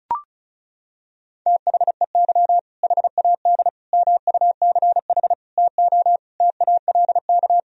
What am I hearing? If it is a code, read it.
Morse code, heard as a keyed tone, about 36 wpm, 700 Hz. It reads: THEY HAD MUCH TO TALK
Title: Sentences from Top 400 Words and Abbreviations 35wpm